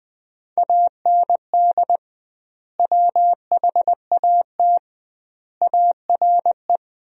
Morse code, heard as a keyed tone, about 20 wpm, 700 Hz. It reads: AND WHAT ARE
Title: 2x Word-Spacing - Sentences from Top 200 Words - 20wpm